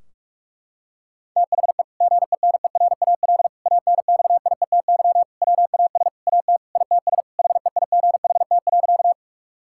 Morse code, heard as a keyed tone, about 46 wpm, 700 Hz. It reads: THE GENERAL ANXIETY WAS AT ITS HEIGHT.
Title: Sentences from Top 1600 Words 45wpm